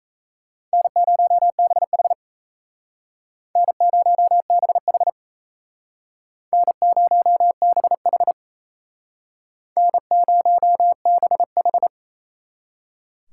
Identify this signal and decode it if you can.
Morse code, heard as a keyed tone, about 35 wpm, 700 Hz. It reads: N0BH N0BH N0BH N0BH
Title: Speed Racing - USA 1x2 Call Signs 28wpm